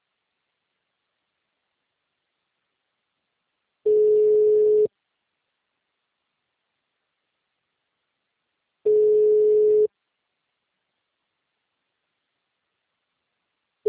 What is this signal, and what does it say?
Ringback tone (Europe, 425 Hz).